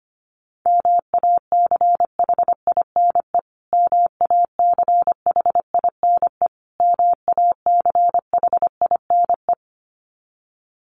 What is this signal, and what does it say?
Morse code, heard as a keyed tone, about 25 wpm, 700 Hz. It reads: MACHINE MACHINE MACHINE
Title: Top 500 Words Repeated 25wpm